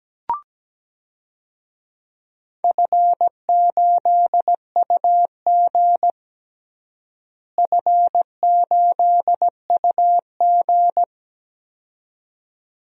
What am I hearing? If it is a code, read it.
Morse code, heard as a keyed tone, about 17 wpm, 700 Hz. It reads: F8UG F8UG
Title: DXCC-100 Call Signs - 4 Characters - Repeated 2x - 17wpm